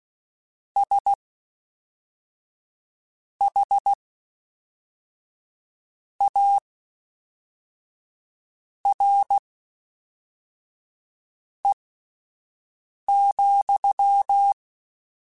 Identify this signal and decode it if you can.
Morse code, heard as a keyed tone, about 16 wpm, spaced as for 4 wpm, 800 Hz. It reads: SHARE,
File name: Misc / Misc-mp3s / WordPairs sp @5